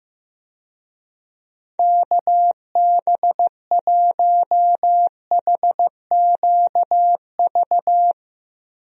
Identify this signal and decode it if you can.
Morse code, heard as a keyed tone, about 15 wpm, 700 Hz. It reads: KB1HQV
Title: USA Call Signs - 6 Characters - Repeated 2x - 15wpm